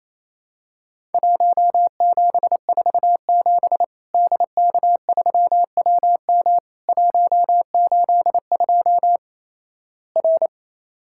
Morse code, heard as a keyed tone, about 28 wpm, 700 Hz. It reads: 1747 DK3WM 182 R
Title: Worked All Europe DX Contest - QTC Exchange - 28wpm